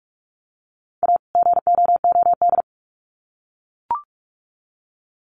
Morse code, heard as a keyed tone, about 44 wpm, 700 Hz. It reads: A GOOD
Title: Sets of 2 Words 45wpm